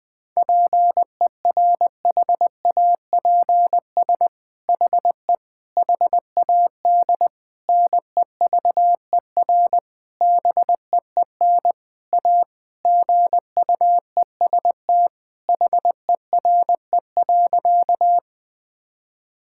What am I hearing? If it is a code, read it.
Morse code, heard as a keyed tone, about 20 wpm, 700 Hz. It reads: PERHAPS HE HAD NEVER BEEN A GUEST HERE.
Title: Sentences from Top 1600 Words 20wpm